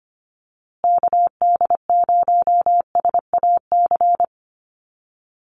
Morse code, heard as a keyed tone, about 25 wpm, 700 Hz. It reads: KD0SAC
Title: USA Call Signs Repeated 2x 25wpm